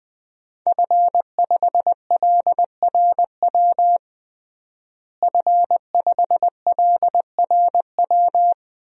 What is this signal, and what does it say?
Morse code, heard as a keyed tone, about 20 wpm, 700 Hz. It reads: F5LRW F5LRW
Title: DXCC-100 Call Signs - 5 Characters - Repeated 2x - 20wpm